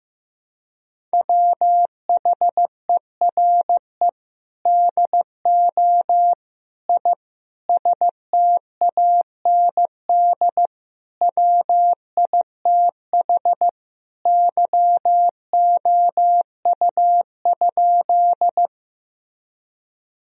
Morse code, heard as a keyed tone, about 15 wpm, 700 Hz. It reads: WHERE DO I STAND WITH YOU?
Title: Sentences from Top 1000 Words - Review Part 1 - 15wpm